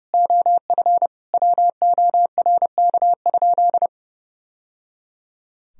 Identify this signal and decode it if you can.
Morse code, heard as a keyed tone, about 30 wpm, 700 Hz. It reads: OF WORK?
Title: Sentences from Top 1000 Words - Review Part 3 - 30wpm